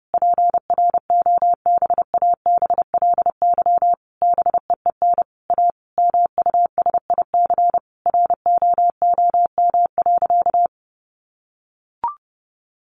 Morse code, heard as a keyed tone, about 30 wpm, 700 Hz. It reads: PROBABLY BEEN A MUSIC ROOM.
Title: Sentences from Top 700 Words 30wpm